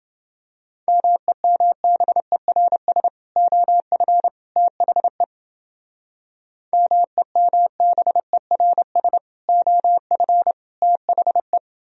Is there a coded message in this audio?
Yes: Morse code from 0.9 to 11.6 s, heard as a keyed tone, about 28 wpm, 700 Hz. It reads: MEMBERS OF THE MEMBERS OF THE